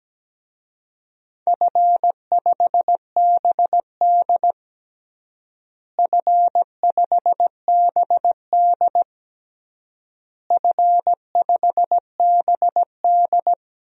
Morse code, heard as a keyed tone, about 17 wpm, 700 Hz. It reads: F5BD F5BD F5BD
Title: DXCC-100 Call Signs - 4 Characters - Repeated 3x - 17wpm